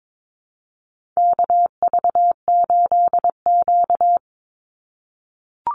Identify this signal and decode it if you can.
Morse code, heard as a keyed tone, about 22 wpm, 700 Hz. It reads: KV8Q
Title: Call Signs - 22wpm